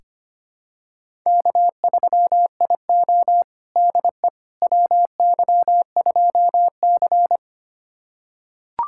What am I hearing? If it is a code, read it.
Morse code, heard as a keyed tone, about 25 wpm, 700 Hz. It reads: K3IO DE WY2C
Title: USA Callsign DE Callsign - 4 Characters - 25wpm